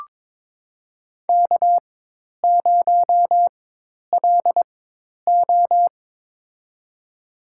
Morse code, heard as a keyed tone, about 22 wpm, spaced as for 10 wpm, 700 Hz. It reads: K0LO
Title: USA Call Signs - 4 Characters - 4x Character Spacing - 22wpm